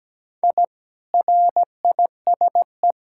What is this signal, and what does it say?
Morse code, heard as a keyed tone, about 17 wpm, 700 Hz. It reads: I RISE
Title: Sentences from Top 1000 Words - Review Part 6 - 17wpm